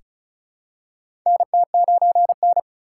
Morse code, heard as a keyed tone, about 36 wpm, 690 Hz. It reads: NT9N
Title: USA Call Signs - 4 Characters - 3x Character Spacing - 35wpm